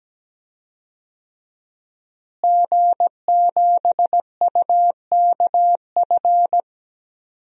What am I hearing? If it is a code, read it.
Morse code, heard as a keyed tone, about 17 wpm, 700 Hz. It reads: G7UKF
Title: DXCC-100 Call Signs - 5 Characters - Repeated 2x - 17wpm